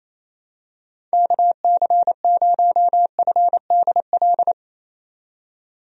Morse code, heard as a keyed tone, about 27 wpm, 700 Hz. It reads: KC0FDL
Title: USA Call Signs - 6 Characters - 28wpm